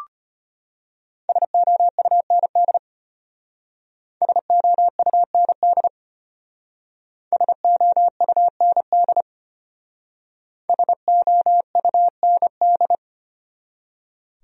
Morse code, heard as a keyed tone, about 33 wpm, 700 Hz. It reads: SOUND SOUND SOUND SOUND